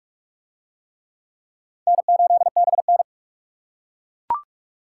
Morse code, heard as a keyed tone, about 40 wpm, 700 Hz. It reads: N8BN